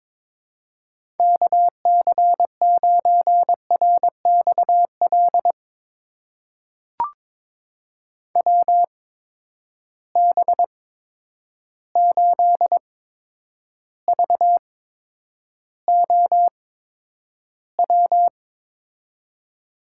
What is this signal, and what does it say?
Morse code, heard as a keyed tone, about 23 wpm, spaced as for 10 wpm, 700 Hz. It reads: KC9RXL WB8VOW
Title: USA Call Signs - 6 Characters - 8x Character Spacing - 22wpm